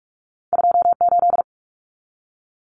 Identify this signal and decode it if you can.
Morse code, heard as a keyed tone, about 46 wpm, 700 Hz. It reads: 28
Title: Numbers - 2-Digit 45wpm